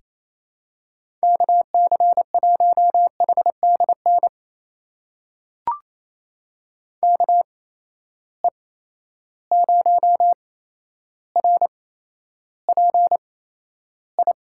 Morse code, heard as a keyed tone, about 28 wpm, spaced as for 12 wpm, 700 Hz. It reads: KC1HDN KE0RPI
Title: USA Call Signs - 8x Character Spacing - 28wpm